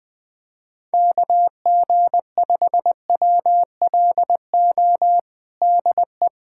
Morse code, heard as a keyed tone, about 21 wpm, 700 Hz. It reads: KG5WLO DE
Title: USA Callsign DE Callsign 20wpm